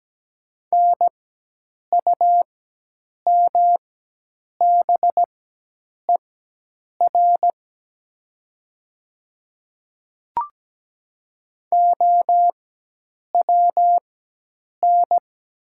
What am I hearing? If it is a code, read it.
Morse code, heard as a keyed tone, about 17 wpm, spaced as for 8 wpm, 700 Hz. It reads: NUMBER OWN
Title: 4x Character Spacing - Top 500 Words - 17wpm